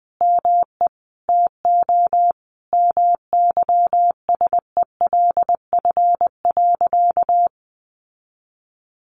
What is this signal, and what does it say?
Morse code, heard as a keyed tone, about 20 wpm, 700 Hz. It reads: ME TO MYSELF.
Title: Sentences from Top 1300 Words 20wpm